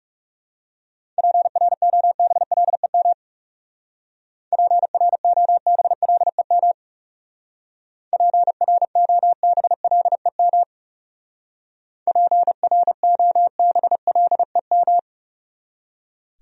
Morse code, heard as a keyed tone, about 37 wpm, 700 Hz. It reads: PROBLEM PROBLEM PROBLEM PROBLEM
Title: Speed-Racing Top 500 Words 30wpm